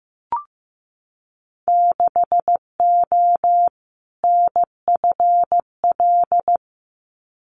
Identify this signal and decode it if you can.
Morse code, heard as a keyed tone, about 15 wpm, 700 Hz. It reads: 6O NFL
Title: Winter Field Day Exchanges 15wpm